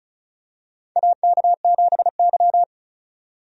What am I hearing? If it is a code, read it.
Morse code, heard as a keyed tone, about 34 wpm, 700 Hz. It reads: AK7Y
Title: USA Call Signs - 4 Characters - 35wpm